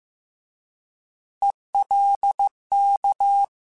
Morse code, heard as a keyed tone, about 15 wpm, 800 Hz. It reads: ELK